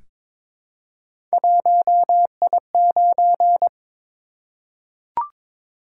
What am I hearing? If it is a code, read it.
Morse code, heard as a keyed tone, about 22 wpm, 700 Hz. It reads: 1I9
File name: Three Letter-Number Combinations - 22wpm